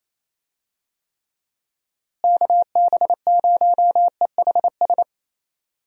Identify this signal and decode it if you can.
Morse code, heard as a keyed tone, about 28 wpm, 700 Hz. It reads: KB0EHS
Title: USA Call Signs - 6 Characters - Repeated 2x - 28wpm